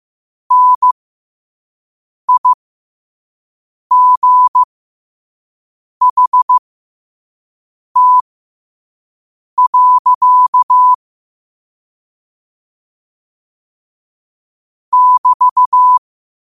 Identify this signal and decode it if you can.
Morse code, heard as a keyed tone, about 15 wpm, spaced as for 5 wpm, 1000 Hz. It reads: NIGHT. <BT>
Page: New quotes every day in morse code at 5 Words per minute.